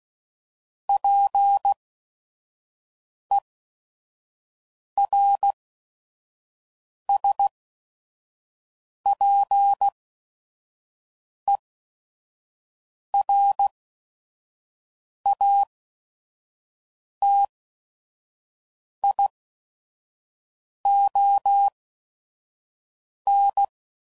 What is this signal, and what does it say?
Morse code, heard as a keyed tone, about 16 wpm, spaced as for 5 wpm, 800 Hz. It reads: PERSPERATION
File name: Sayings / Quotes / Persperation